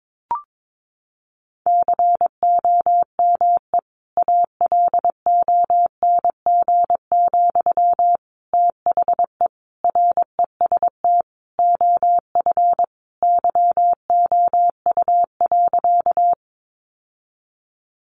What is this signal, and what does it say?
Morse code, heard as a keyed tone, about 22 wpm, 700 Hz. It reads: COME ALONG, THE REST OF YOU.
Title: Sentences from Top 700 Words 22wpm